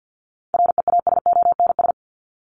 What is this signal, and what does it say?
Morse code, heard as a keyed tone, about 50 wpm, 700 Hz. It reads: REASONS
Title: ES:S Suffix - Top 100 Words with Base Words 50wpm